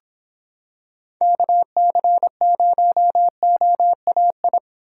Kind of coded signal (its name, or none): Morse code